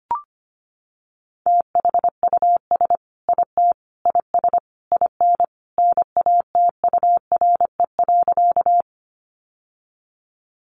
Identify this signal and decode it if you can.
Morse code, heard as a keyed tone, about 25 wpm, 700 Hz. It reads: THUS IT IS IN NATURE.
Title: Sentences from Top 800 Words 25wpm